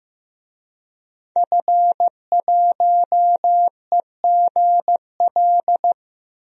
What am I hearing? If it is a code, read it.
Morse code, heard as a keyed tone, about 15 wpm, 700 Hz. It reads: F1EGL
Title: DXCC-100 Call Signs - 5 Characters - Repeated 2x - 15wpm